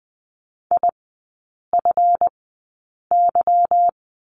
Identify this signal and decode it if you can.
Morse code, heard as a keyed tone, about 20 wpm, spaced as for 8 wpm, 700 Hz. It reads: IFY
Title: 178 Lesson 29e - Slash - Three Characters 20wpm